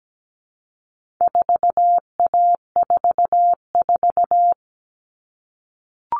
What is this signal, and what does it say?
Morse code, heard as a keyed tone, about 17 wpm, 700 Hz. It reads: 4A44
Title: All Cut Numbers - 4-Digit 17wpm